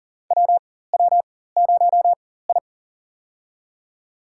Morse code, heard as a keyed tone, about 40 wpm, spaced as for 19 wpm, 700 Hz. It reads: WW0I
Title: USA Call Signs - 4 Characters - 4x Character Spacing - 40wpm